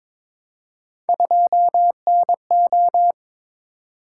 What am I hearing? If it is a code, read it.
Morse code, heard as a keyed tone, about 22 wpm, 700 Hz. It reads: 2NO
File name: practice / 22wpm / Three Letter-Number Combinations - 22wpm